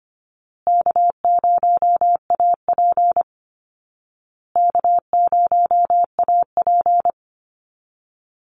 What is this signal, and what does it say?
Morse code, heard as a keyed tone, about 25 wpm, 700 Hz. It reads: K0AP K0AP
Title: Call Signs Repeated 2x - 25wpm